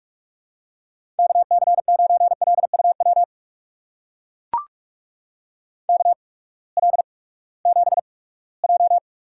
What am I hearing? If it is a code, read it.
Morse code, heard as a keyed tone, about 44 wpm, spaced as for 22 wpm, 700 Hz. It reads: KC9LUW KL7J
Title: USA Call Signs - 6 Characters - 8x Character Spacing - 45wpm